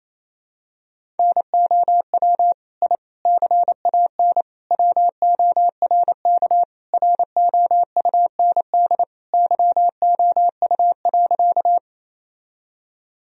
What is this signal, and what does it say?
Morse code, heard as a keyed tone, about 28 wpm, 700 Hz. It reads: NOW I CAN WORK ROUND YOU.
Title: Sentences from Top 1000 Words - Review Part 1 - 28wpm